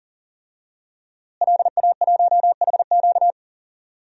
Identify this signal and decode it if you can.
Morse code, heard as a keyed tone, about 40 wpm, 700 Hz. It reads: LA1HQ
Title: DXCC-100 Call Signs - 5 Characters - Repeated 3x - 40wpm